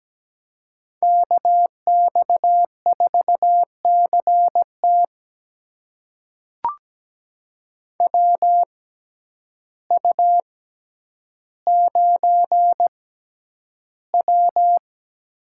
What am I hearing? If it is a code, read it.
Morse code, heard as a keyed tone, about 17 wpm, spaced as for 9 wpm, 700 Hz. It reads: KX4CT WU9W
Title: USA Call Signs - 6x Character Spacing - 17wpm